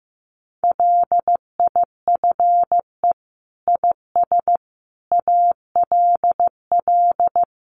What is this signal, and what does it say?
Morse code, heard as a keyed tone, about 15 wpm, 700 Hz. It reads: LIFE IS ALL
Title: Sentences from Top 300 Words 15wpm